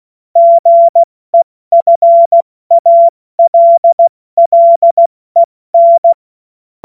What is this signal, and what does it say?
Morse code, heard as a keyed tone, about 16 wpm, 680 Hz. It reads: GEFALLEN
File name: Morsekurs / Witze / Witz 03. August 2021, 70 F80